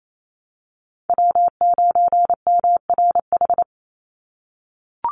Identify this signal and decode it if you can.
Morse code, heard as a keyed tone, about 28 wpm, 700 Hz. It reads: W9MRH
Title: NA SOTA Chaser Callsigns 28wpm